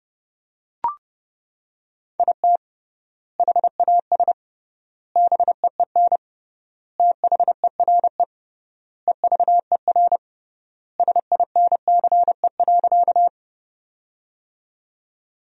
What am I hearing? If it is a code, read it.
Morse code, heard as a keyed tone, about 30 wpm, 700 Hz. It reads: IT HAS BEEN THERE EVER SINCE.
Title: 3x Word-Spacing - Sentences from Top 400 Words - 30wpm